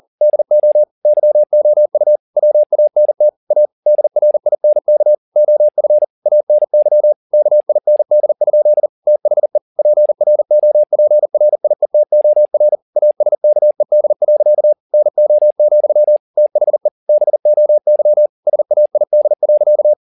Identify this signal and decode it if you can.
Morse code, heard as a keyed tone, about 40 wpm, 600 Hz. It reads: DO YOU WANT A DRINK OF ANY KIND? THE PROPRIETOR ASKED. NO, THE BOY SAID.
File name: the old man and the sea 2 40wpm